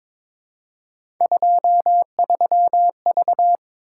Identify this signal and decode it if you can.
Morse code, heard as a keyed tone, about 22 wpm, 700 Hz. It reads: 23V
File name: practice / 22wpm / Three Letter-Number Combinations - 22wpm